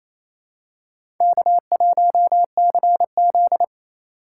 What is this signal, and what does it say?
Morse code, heard as a keyed tone, about 28 wpm, 700 Hz. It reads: K1CZ
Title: USA Call Signs - 4 Characters - 4x Character Spacing - 28wpm